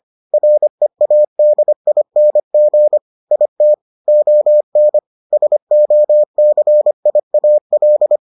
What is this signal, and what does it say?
Morse code, heard as a keyed tone, about 25 wpm, 600 Hz. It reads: READING IT ON SOCIAL